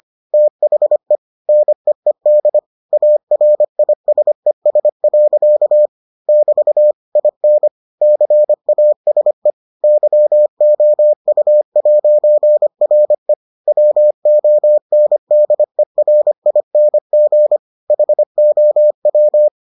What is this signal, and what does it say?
Morse code, heard as a keyed tone, about 25 wpm, 600 Hz. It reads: THE NEED ARISES. <BT> IN CASE YOU'RE WONDERING HOW